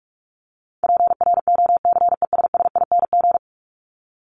Morse code, heard as a keyed tone, about 44 wpm, 700 Hz. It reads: PROCESSING
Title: ING Suffix - Top 100 Words with Base Words 45wpm